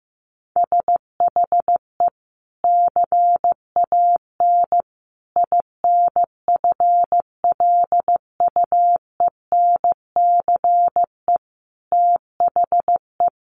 Morse code, heard as a keyed tone, about 15 wpm, 700 Hz. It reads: SHE CAN INFLUENCE THE
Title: Sentences from Top 1100 Words 15wpm